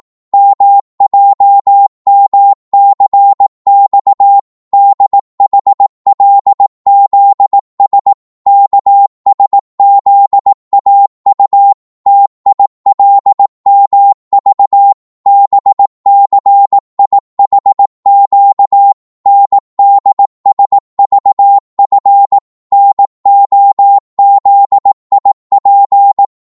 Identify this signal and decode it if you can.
Morse code, heard as a keyed tone, about 18 wpm, 800 Hz. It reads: MJMCX DHLZS KSZAU TILMV BCIHQ NDSVF NOZIP
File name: Morse / ltr-18wpm